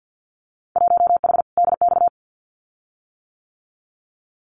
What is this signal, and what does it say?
Morse code, heard as a keyed tone, about 50 wpm, 700 Hz. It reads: 1H DX
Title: Winter Field Day Exchanges 50wpm